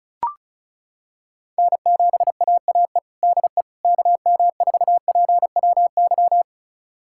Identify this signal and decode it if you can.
Morse code, heard as a keyed tone, about 34 wpm, 700 Hz. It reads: N7AAE DE KM4PWY
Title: USA Callsign DE Callsign 35wpm